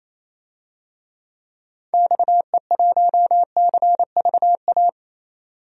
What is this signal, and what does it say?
Morse code, heard as a keyed tone, about 28 wpm, 700 Hz. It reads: XE1CVA
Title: DXCC-100 Call Signs - 6 Characters - 28wpm